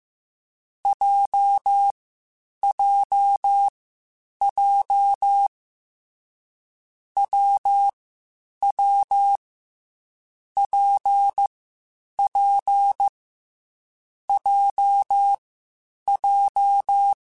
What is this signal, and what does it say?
Morse code, heard as a keyed tone, about 15 wpm, 800 Hz. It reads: J J J W W P P J J
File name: Difficult Combos / Difficult Combos-mp3s / WPJ WPJ1 @13